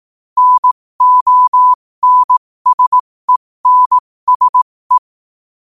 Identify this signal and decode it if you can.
Morse code, heard as a keyed tone, about 18 wpm, spaced as for 15 wpm, 1000 Hz. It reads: NONSENSE